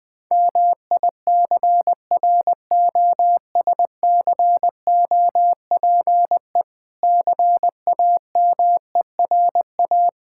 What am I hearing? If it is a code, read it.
Morse code, heard as a keyed tone, about 20 wpm, 700 Hz. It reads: MICROSCOPE CAMERA
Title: Difficult Words - Speed-Racing with Hints - 20wpm